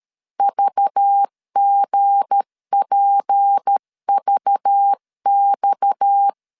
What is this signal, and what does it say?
Morse code, heard as a keyed tone, about 13 wpm, 800 Hz. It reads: VGPVX
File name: LEZIONI / 7 GF9Z / 7 GF 12wpm 2122